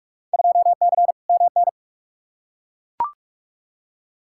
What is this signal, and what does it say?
Morse code, heard as a keyed tone, about 42 wpm, 700 Hz. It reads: 2C MN
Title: Field Day Exchanges Encore - 45wpm